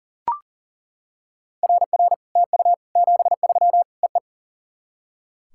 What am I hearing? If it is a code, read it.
Morse code, heard as a keyed tone, about 41 wpm, 700 Hz. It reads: RR TU 73 EE
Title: SOTA QSO Practice 40wpm